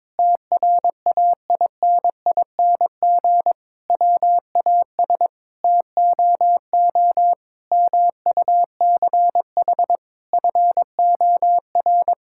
Morse code, heard as a keyed tone, about 22 wpm, 700 Hz. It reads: TRAINING WAS TOO MUCH FOR
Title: Sentences from Top 1000 Words - Review Part 5 - 22wpm